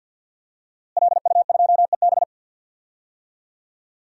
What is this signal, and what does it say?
Morse code, heard as a keyed tone, about 51 wpm, 700 Hz. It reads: LU2EB
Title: DXCC-100 Call Signs - 5 Characters - Repeated 3x - 50wpm